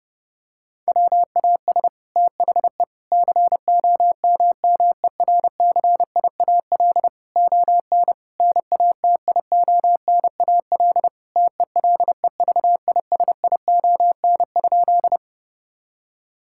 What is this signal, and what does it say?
Morse code, heard as a keyed tone, about 32 wpm, 700 Hz. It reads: WAS THE COMMERCIAL ON NATIONAL TELEVISION?
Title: Sentences from Top 1000 Words - Review Part 5 - 30wpm